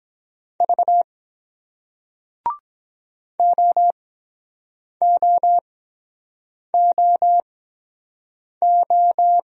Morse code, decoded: V OOOO

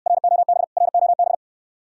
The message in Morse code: ROB ROB

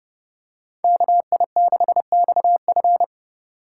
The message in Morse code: KI6XF

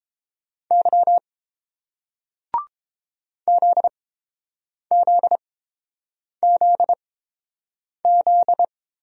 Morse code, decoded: Y ZZZZ

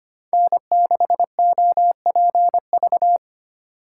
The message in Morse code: N6OPV

